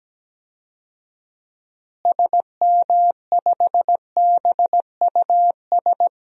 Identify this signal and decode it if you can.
Morse code, heard as a keyed tone, about 17 wpm, 700 Hz. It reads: SM5BUS
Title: DXCC-100 Call Signs - 6 Characters - Repeated 2x - 17wpm